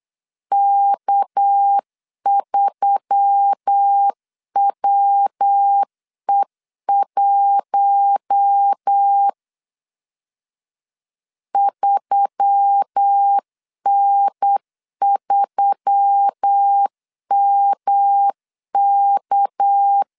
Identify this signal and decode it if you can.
Morse code, heard as a keyed tone, about 8 wpm, 800 Hz. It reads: K3WE1 3N3MK
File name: CW / DET / 1 6 1500C S C 8wpm 5219